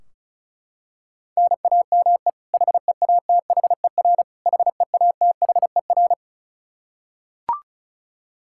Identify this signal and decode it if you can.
Morse code, heard as a keyed tone, about 36 wpm, 700 Hz. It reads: NAME HEATHER HEATHER
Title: K1USN SST QSO Practice 35wpm